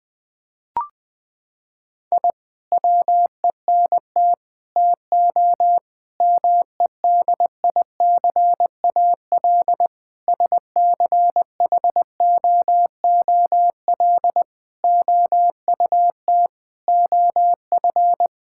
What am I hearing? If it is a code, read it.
Morse code, heard as a keyed tone, about 20 wpm, 700 Hz. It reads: I WENT TO MEDICAL SCHOOL OUT OF
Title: Sentences from Top 1000 Words - Review Part 5 - 20wpm